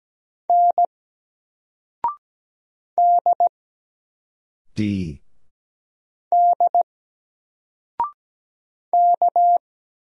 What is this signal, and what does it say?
Morse code, heard as a keyed tone, about 17 wpm, 700 Hz. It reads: N D D K
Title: Single Letter-Number Alt 17wpm